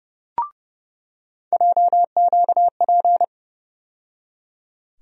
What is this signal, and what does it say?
Morse code, heard as a keyed tone, about 30 wpm, 700 Hz. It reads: JQP
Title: USA Call Sign Postfixes - 3 Letters 30wpm